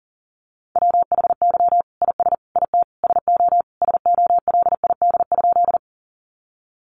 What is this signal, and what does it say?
Morse code, heard as a keyed tone, about 40 wpm, 700 Hz. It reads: WHY IS IT SO SOLID?
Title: Sentences from Top 1500 Words 40wpm